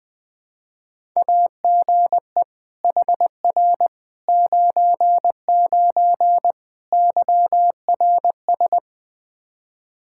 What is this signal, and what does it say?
Morse code, heard as a keyed tone, about 20 wpm, 700 Hz. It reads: AGE HR 99 YRS
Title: Practice QSO - Exchange 3 - More Info - 20wpm